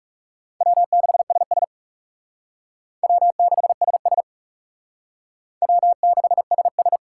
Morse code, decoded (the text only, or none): W6SS W6SS W6SS